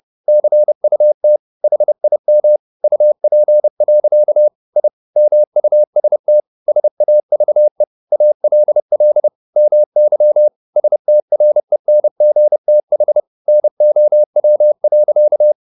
Morse code, heard as a keyed tone, about 30 wpm, 600 Hz. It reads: CUT HIM UP. I MUST SAVE ALL MY STRENGTH NOW.
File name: the old man and the sea / the old man and the sea 1 30wpm